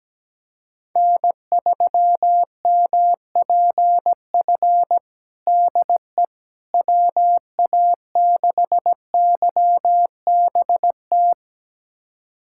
Morse code, heard as a keyed tone, about 17 wpm, 700 Hz. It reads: N3MPF DE WA6YBT
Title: USA Callsign DE Callsign 17wpm